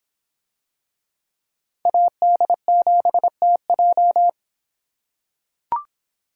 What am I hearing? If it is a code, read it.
Morse code, heard as a keyed tone, about 24 wpm, 700 Hz. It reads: AD7TJ